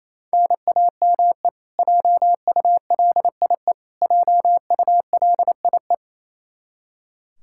Morse code, heard as a keyed tone, about 28 wpm, 700 Hz. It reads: NAME JULIE JULIE
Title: Practice QSO - Exchange 1 - Beginning - 28wpm